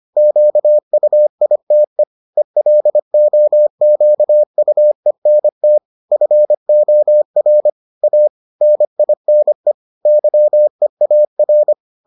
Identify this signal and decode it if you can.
Morse code, heard as a keyed tone, about 25 wpm, 600 Hz. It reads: QUITE ELOQUENT FOR A NINE YEAR